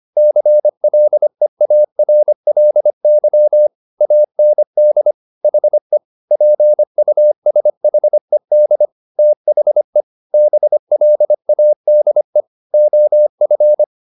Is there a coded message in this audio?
Yes: Morse code from 0.2 to 13.8 s, heard as a keyed tone, about 25 wpm, 600 Hz. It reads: CLEARLY AND HE PUSHED THE BLADE OF